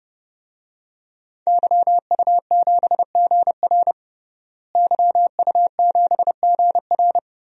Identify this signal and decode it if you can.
Morse code, heard as a keyed tone, about 30 wpm, 700 Hz. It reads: YU7GR YU7GR
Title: DXCC-100 Call Signs - 5 Characters - Repeated 2x - 30wpm